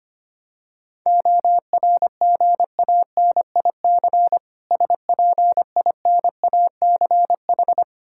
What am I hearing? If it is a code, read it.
Morse code, heard as a keyed tone, about 25 wpm, 690 Hz. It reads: ORGANIC SPINACH
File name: practice / 25wpm / Difficult Words with Hints - 25wpm